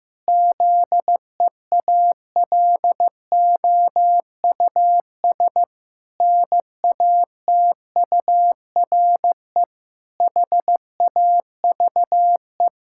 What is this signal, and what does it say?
Morse code, heard as a keyed tone, about 15 wpm, 700 Hz. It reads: ZEALOUS NATURE HAVE
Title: Pangram Challenge v2 - 15wpm